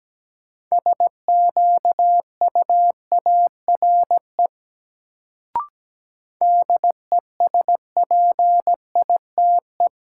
Morse code, heard as a keyed tone, about 17 wpm, 700 Hz. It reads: SQUARE DESPITE
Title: Sentences from Top 1000 Words - Review Part 7 - New Word Drill - 17wpm